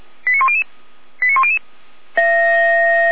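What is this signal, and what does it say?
Five-tone selective call (ZVEI), heard as quick runs of five tones, tones of about 70 ms. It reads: 89100 89100
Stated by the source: ZVEI five-tone selective call received on an SDR, channel 455.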